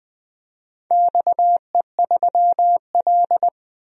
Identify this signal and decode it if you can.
Morse code, heard as a keyed tone, about 20 wpm, 700 Hz. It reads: XE3L
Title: DXCC-100 Call Signs - 4 Characters - 20wpm